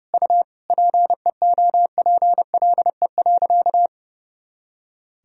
Morse code, heard as a keyed tone, about 32 wpm, 700 Hz. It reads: U PEOPLE.
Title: Sentences from Top 300 Words and Abbreviations 30wpm